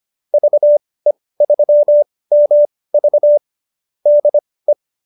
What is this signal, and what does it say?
Morse code, heard as a keyed tone, about 25 wpm, spaced as for 18 wpm, 590 Hz. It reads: VE3MV DE